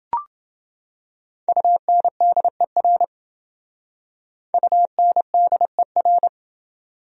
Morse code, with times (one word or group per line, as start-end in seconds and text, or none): 1.5-3.0 UNDER
4.5-6.3 UNDER